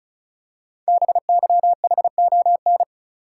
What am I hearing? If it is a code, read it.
Morse code, heard as a keyed tone, about 36 wpm, 700 Hz. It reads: BYHON
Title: Code Groups - 5 Letters - 35wpm